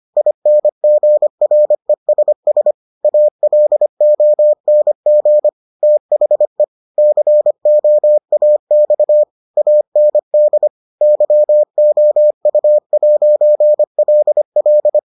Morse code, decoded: INGRESS ALONG THE COAX AND YOU'LL

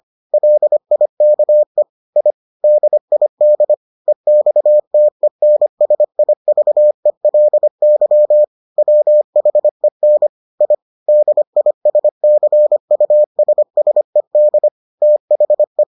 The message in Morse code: LIKE I DID EXTENSIVELY WHEN I DISCUSSED THE

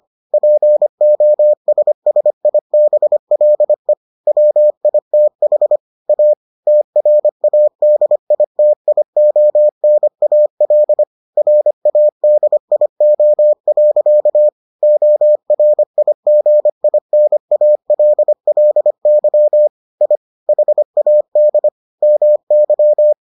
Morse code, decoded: POSSIBLE WITH A TRADITIONAL RADIO. ORIGINALLY I HAD MY

